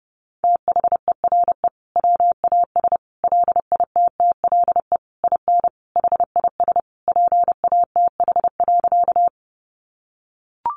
Morse code, heard as a keyed tone, about 30 wpm, 700 Hz. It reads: THERE WAS LITTLE IN HIS PATH.
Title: Sentences from Top 1400 Words 30wpm